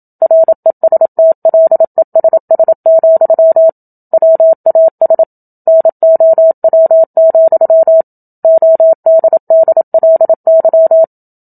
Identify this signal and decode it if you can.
Morse code, heard as a keyed tone, about 27 wpm, 650 Hz. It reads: RESTLESS, WAS NOW, ODDLY
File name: Book Two / 01-Chapter 1 - 27 WPM